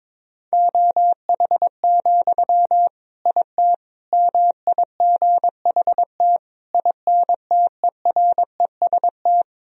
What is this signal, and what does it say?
Morse code, heard as a keyed tone, about 22 wpm, 700 Hz. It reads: OH, IT MIGHT INTEREST
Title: Sentences from Top 1000 Words - Review Part 2 - 22wpm